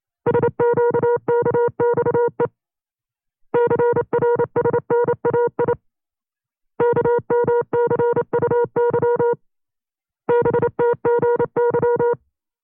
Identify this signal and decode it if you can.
Morse code, heard as a keyed tone, about 29 wpm, 480 Hz. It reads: SQKXE CRSNAI KMCUY BTGY
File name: FreeFiles / CW-practice / 25WPM-side2